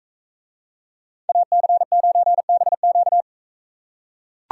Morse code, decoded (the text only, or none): AC9BQ